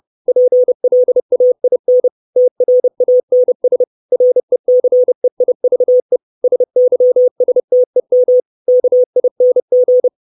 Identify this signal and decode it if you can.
Morse code, heard as a keyed tone, about 30 wpm, 500 Hz. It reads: PLAIN TRANS RECEIVE SYSTEM KING
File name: 632 Words 30WPM